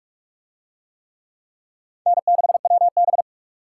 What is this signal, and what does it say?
Morse code, heard as a keyed tone, about 42 wpm, 700 Hz. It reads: N6WB